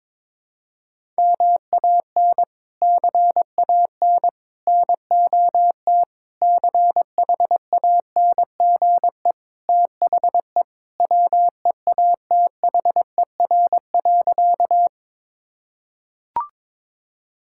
Morse code, heard as a keyed tone, about 21 wpm, 700 Hz. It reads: MAN CAN NOT CHANGE THE WEATHER.
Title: Sentences from Top 1000 Words - Review Part 5 - 22wpm